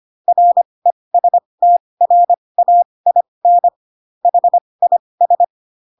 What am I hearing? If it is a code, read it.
Morse code, heard as a keyed tone, about 25 wpm, spaced as for 20 wpm, 700 Hz. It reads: RESTRAIN HIS
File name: Pinocchio / Pinocchio ch8 15 25 ik1hge